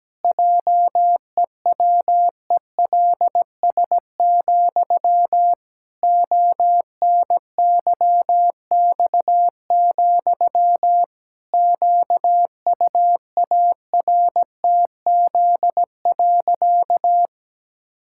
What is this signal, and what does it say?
Morse code, heard as a keyed tone, about 17 wpm, 700 Hz. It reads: JEWELS, ONYX, QUARTZ.
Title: Pangram Challenge v2 - 17wpm